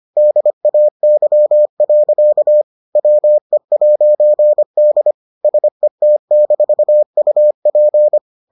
Morse code, heard as a keyed tone, about 25 wpm, 600 Hz. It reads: DAY. WE'D SET-UP